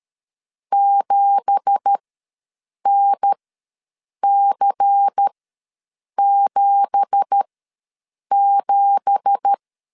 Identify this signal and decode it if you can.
Morse code, heard as a keyed tone, about 13 wpm, spaced as for 7 wpm, 800 Hz. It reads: 7NC77
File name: LEZIONI / 9 2867 ws 12wpm 3431